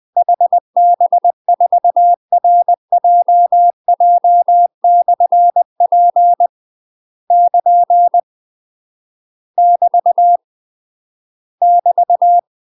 Morse code, heard as a keyed tone, about 20 wpm, 700 Hz. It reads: HB4RJJ/P <KN> <BT> <BT>
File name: UK MASTER QSOs TRAINING / 20WPM / UK MASTER QSOs 0048